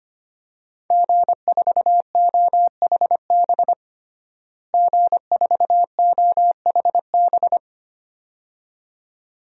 Morse code, heard as a keyed tone, about 25 wpm, 700 Hz. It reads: G4OHB G4OHB
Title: DXCC-100 Call Signs - 5 Characters - Repeated 2x - 25wpm